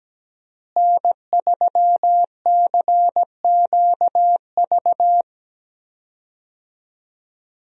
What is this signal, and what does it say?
Morse code, heard as a keyed tone, about 17 wpm, 700 Hz. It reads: N3CQV
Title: USA Call Signs - 5 Characters - Repeated 2x - 17wpm